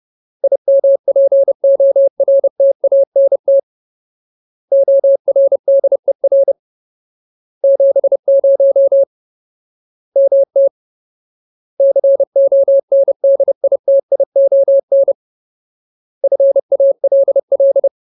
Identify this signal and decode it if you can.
Morse code, heard as a keyed tone, about 30 wpm, 550 Hz. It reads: IMPORTANT ORDER 70 MT CONDITION FALL
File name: Ultimate Word List / 30WPM Ultimate